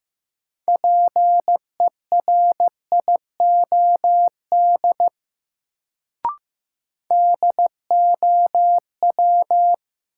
Morse code, heard as a keyed tone, about 16 wpm, 700 Hz. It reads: PERIOD DOW
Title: Sets of 5 Words - New Word Drill - 15wpm